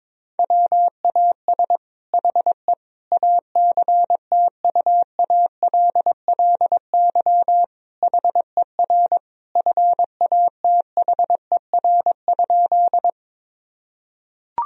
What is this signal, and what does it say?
Morse code, heard as a keyed tone, about 22 wpm, 700 Hz. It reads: WAS HE ACTUALLY HER FATHER?
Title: Sentences from Top 1000 Words - Review Part 4 - 22wpm